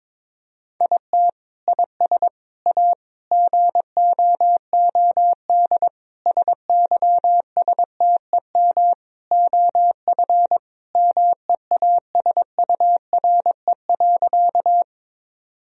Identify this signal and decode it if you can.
Morse code, heard as a keyed tone, about 22 wpm, 700 Hz. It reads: IT IS A GOOD SYSTEM OF MEASURE.
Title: Sentences from Top 1000 Words - Review Part 3 - 22wpm